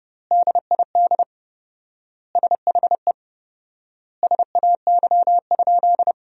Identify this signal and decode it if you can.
Morse code, heard as a keyed tone, about 29 wpm, 700 Hz. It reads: DID SHE SAY?
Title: 4x Word-Spacing - Sentences from Top 500 Words - 30wpm